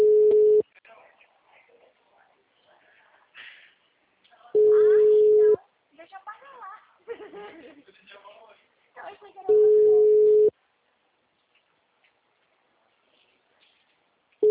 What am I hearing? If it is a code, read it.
Ringback tone (Europe, 425 Hz), with speech.
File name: out- -2025- - - .99582